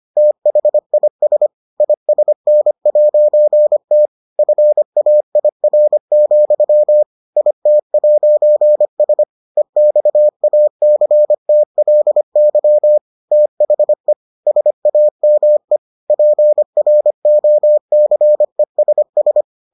Morse code, heard as a keyed tone, about 25 wpm, 600 Hz. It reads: THIS ISN'T FAIR, IT'S EXACTLY THE SAME PROCESS